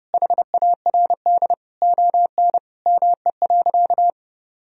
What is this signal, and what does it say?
Morse code, heard as a keyed tone, about 30 wpm, 700 Hz. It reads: HARD ON ME.
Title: Sentences from Top 1000 Words - Review Part 1 - 30wpm